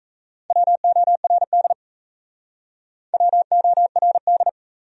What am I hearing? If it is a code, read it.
Morse code, heard as a keyed tone, about 41 wpm, 700 Hz. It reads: WORD WORD